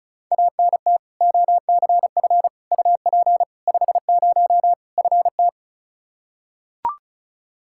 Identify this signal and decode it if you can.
Morse code, heard as a keyed tone, about 34 wpm, 700 Hz. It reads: ANT OCF UP 50 FT
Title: Practice QSO - Exchange 2 - Info - 35wpm